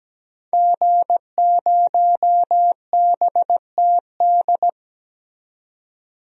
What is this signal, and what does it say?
Morse code, heard as a keyed tone, about 17 wpm, 700 Hz. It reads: G0BTD
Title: DXCC-100 Call Signs - 5 Characters - 17wpm